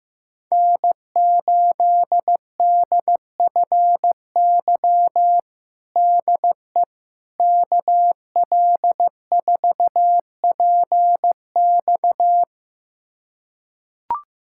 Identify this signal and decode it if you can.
Morse code, heard as a keyed tone, about 16 wpm, 700 Hz. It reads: N8DFY DE KL4PX
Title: USA Callsign DE Callsign - 5 Characters - 15wpm